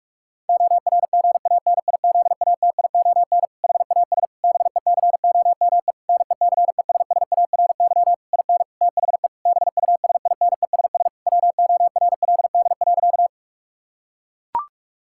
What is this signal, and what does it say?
Morse code, heard as a keyed tone, about 41 wpm, 700 Hz. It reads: ORGANIZATION HAS BECOME NECESSARY IN THE BUSINESS WORLD.
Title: Sentences from Top 1000 Words - Review Part 5 - 45wpm